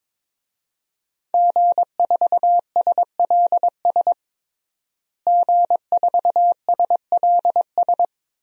Morse code, heard as a keyed tone, about 22 wpm, 700 Hz. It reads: G4SLS G4SLS
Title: DXCC-100 Call Signs - 5 Characters - Repeated 3x - 22wpm